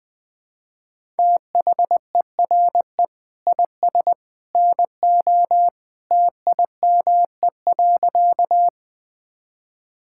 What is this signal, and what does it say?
Morse code, heard as a keyed tone, about 20 wpm, 700 Hz. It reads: THERE IS NO TIME.
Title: Sentences with 2-Word Phrases - 20wpm